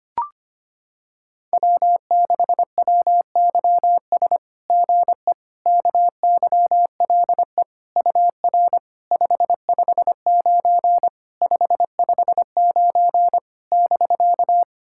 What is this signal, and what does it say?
Morse code, heard as a keyed tone, about 25 wpm, 700 Hz. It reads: W6WYS GE KYLE UR 559 559 <BK>
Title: SOTA QSO Practice 25wpm